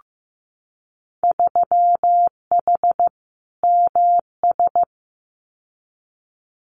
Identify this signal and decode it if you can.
Morse code, heard as a keyed tone, about 15 wpm, 700 Hz. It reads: 3H MS
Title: Winter Field Day Exchanges Encore 15wpm